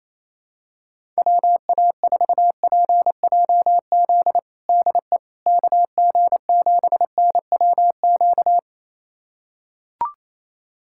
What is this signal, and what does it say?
Morse code, heard as a keyed tone, about 28 wpm, 700 Hz. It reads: WA4PJZ DE KG7NWQ
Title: USA Callsign DE Callsign 28wpm